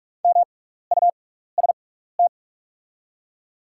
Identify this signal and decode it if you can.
Morse code, heard as a keyed tone, about 44 wpm, spaced as for 15 wpm, 700 Hz. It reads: MUST
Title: 6x Character Spacing - Top 200 Words - 45wpm